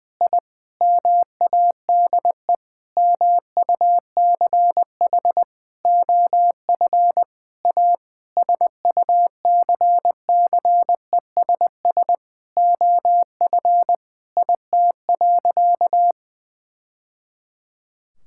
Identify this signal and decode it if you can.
Morse code, heard as a keyed tone, about 21 wpm, 700 Hz. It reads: I MADE MUCH OF A SUCCESS OF IT.
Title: Sentences from Top 1000 Words - Review Part 6 - 20wpm